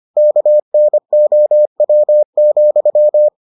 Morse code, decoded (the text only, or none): KNOW,